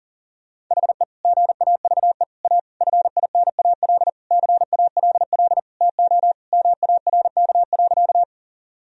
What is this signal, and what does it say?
Morse code, heard as a keyed tone, about 40 wpm, 700 Hz. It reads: HE GAVE A FINAL CALL TO MARK.
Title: Sentences from Top 1000 Words - Review Part 2 - 40wpm